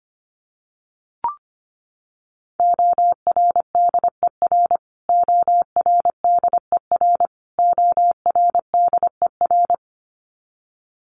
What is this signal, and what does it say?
Morse code, heard as a keyed tone, about 25 wpm, 700 Hz. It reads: ORDER ORDER ORDER
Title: Top 500 Words Repeated 25wpm